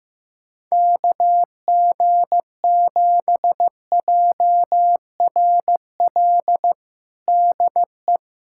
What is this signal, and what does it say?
Morse code, heard as a keyed tone, about 15 wpm, 700 Hz. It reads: KG7JRL DE
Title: USA Callsign DE Callsign - 6 Characters - 15wpm